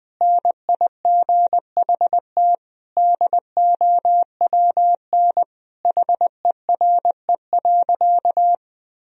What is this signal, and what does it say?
Morse code, heard as a keyed tone, about 21 wpm, 700 Hz. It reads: NIGHT DOWN HERE.
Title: Sentences from Top 1000 Words - Review Part 2 - 20wpm